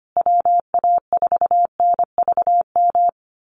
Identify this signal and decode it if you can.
Morse code, heard as a keyed tone, about 25 wpm, 700 Hz. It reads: WA4NVM
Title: USA Call Signs Repeated 2x 25wpm